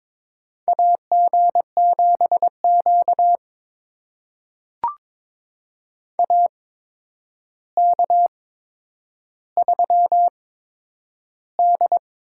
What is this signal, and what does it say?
Morse code, heard as a keyed tone, about 22 wpm, spaced as for 9 wpm, 700 Hz. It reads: AG7Q AK3D
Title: USA Call Signs - 4 Characters - 8x Character Spacing - 22wpm